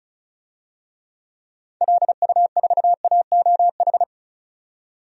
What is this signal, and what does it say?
Morse code, heard as a keyed tone, about 34 wpm, 700 Hz. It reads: LU4AOH
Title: DXCC-100 Call Signs - 6 Characters - 35wpm